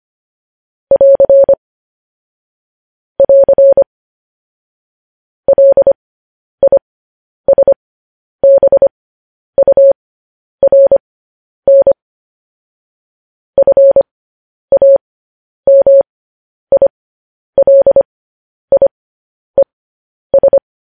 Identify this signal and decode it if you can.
Morse code, heard as a keyed tone, about 25 wpm, spaced as for 10 wpm, 550 Hz. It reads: <AR> <AR> LISBURN FAMILIES
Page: News Headlines in Morse Code at 10 WPM